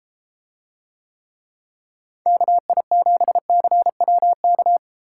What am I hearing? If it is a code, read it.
Morse code, heard as a keyed tone, about 32 wpm, 700 Hz. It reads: KI7CWK